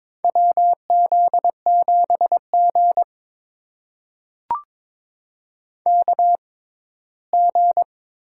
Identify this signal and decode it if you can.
Morse code, heard as a keyed tone, about 22 wpm, spaced as for 15 wpm, 700 Hz. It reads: WZ7G KG
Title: USA Call Signs - 4 Characters - 6x Character Spacing - 22wpm